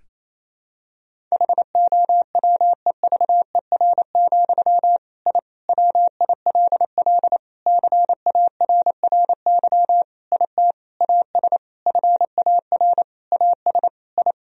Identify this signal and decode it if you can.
Morse code, heard as a keyed tone, about 27 wpm, 700 Hz. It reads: HOWEVER, I WILL CARRY IT AS FAR AS I
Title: Sentences from Top 1000 Words - Review Part 1 - 28wpm